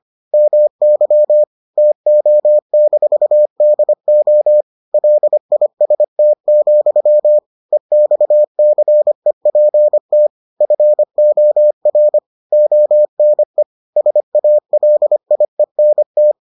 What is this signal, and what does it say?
Morse code, heard as a keyed tone, about 25 wpm, 600 Hz. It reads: MY TO-DO LIST, EXCEPT FOR ONE SALIENT